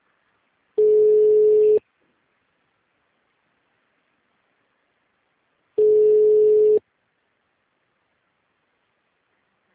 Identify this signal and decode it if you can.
Ringback tone (Europe, 425 Hz).
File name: out- -2032- - - .28318